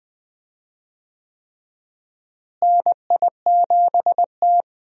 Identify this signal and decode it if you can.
Morse code, heard as a keyed tone, about 21 wpm, 700 Hz. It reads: NI7T
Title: USA Call Signs - 4 Characters - Repeated 3x - 20wpm